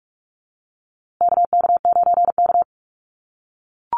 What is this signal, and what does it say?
Morse code, heard as a keyed tone, about 44 wpm, 700 Hz. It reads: KK9K
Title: USA Call Signs Repeated 3x 45wpm